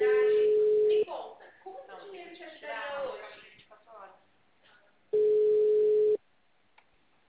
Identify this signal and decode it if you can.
Ringback tone (Europe, 425 Hz), with speech.